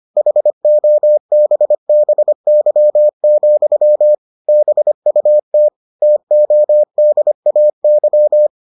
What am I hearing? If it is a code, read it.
Morse code, heard as a keyed tone, about 25 wpm, 600 Hz. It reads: HOBBY, BUT TODAY